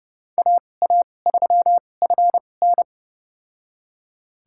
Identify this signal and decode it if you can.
Morse code, heard as a keyed tone, about 32 wpm, spaced as for 22 wpm, 700 Hz. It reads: AA3FN
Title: USA Call Signs - 5 Characters - 2x Character Spacing - 30wpm